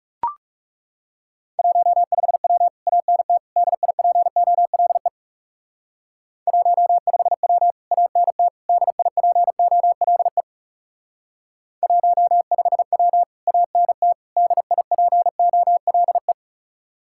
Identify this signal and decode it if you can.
Morse code, heard as a keyed tone, about 39 wpm, 700 Hz. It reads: 15W ANT DIPOLE 15W ANT DIPOLE 15W ANT DIPOLE